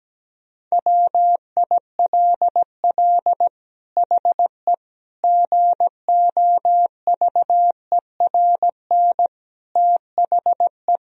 Morse code, heard as a keyed tone, about 17 wpm, 700 Hz. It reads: WILL HE GOVERN THE
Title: Sentences from Top 1000 Words - Review Part 2 - 17wpm